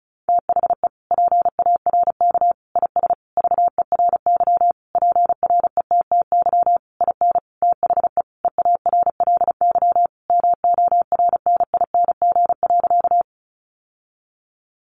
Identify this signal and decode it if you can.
Morse code, heard as a keyed tone, about 34 wpm, 700 Hz. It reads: THE PARK IS VERY PRETTY IN THE EARLY MORNING.
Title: Sentences from Top 1300 Words 35wpm